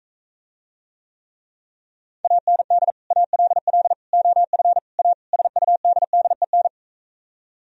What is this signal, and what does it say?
Morse code, heard as a keyed tone, about 45 wpm, 700 Hz. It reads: AND ALL OF A SUDDEN